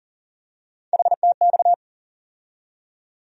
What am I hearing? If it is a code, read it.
Morse code, heard as a keyed tone, about 40 wpm, 700 Hz. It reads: HTX
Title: USA Call Sign Postfixes - 3 Letters 40wpm